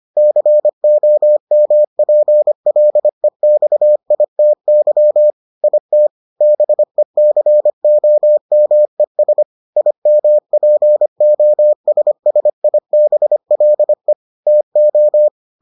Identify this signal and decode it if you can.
Morse code, heard as a keyed tone, about 25 wpm, 600 Hz. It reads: COMPLEXITY IT BECOMES IMPOSSIBLE TO